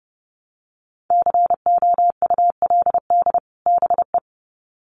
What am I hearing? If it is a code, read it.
Morse code, heard as a keyed tone, about 30 wpm, 700 Hz. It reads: COULD BE
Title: Sets of 2 Words 30wpm